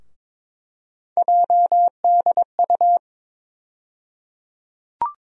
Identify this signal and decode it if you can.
Morse code, heard as a keyed tone, about 22 wpm, 700 Hz. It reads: JDU
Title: USA Call Sign Postfixes - 3 Letters 22wpm